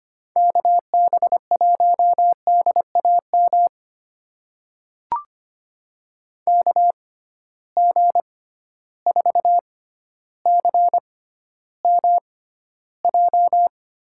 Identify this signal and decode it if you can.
Morse code, heard as a keyed tone, about 25 wpm, spaced as for 13 wpm, 700 Hz. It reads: KB1DAM KG4CMJ
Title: USA Call Signs - 6 Characters - 6x Character Spacing - 25wpm